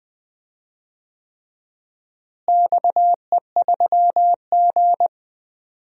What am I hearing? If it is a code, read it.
Morse code, heard as a keyed tone, about 21 wpm, 700 Hz. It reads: XE3G
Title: DXCC-100 Call Signs - 4 Characters - Repeated 3x - 20wpm